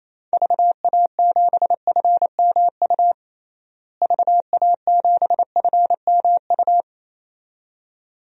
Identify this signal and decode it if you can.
Morse code, heard as a keyed tone, about 28 wpm, 700 Hz. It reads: VA7FMU VA7FMU
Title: DXCC-100 Call Signs - 6 Characters - Repeated 2x - 28wpm